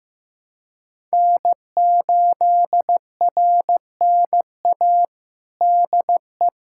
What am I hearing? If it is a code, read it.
Morse code, heard as a keyed tone, about 15 wpm, 700 Hz. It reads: N8RNA DE
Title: USA Callsign DE Callsign 15wpm